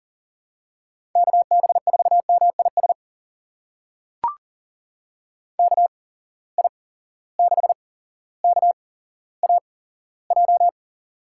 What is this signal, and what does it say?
Morse code, heard as a keyed tone, about 39 wpm, spaced as for 17 wpm, 700 Hz. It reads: KB4MIS KI6KAJ
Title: USA Call Signs - 6 Characters - 8x Character Spacing - 40wpm